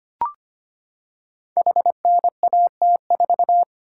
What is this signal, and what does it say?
Morse code, heard as a keyed tone, about 25 wpm, 700 Hz. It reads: HNAT4
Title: Code Groups - 5 Letter:Numbers - 25wpm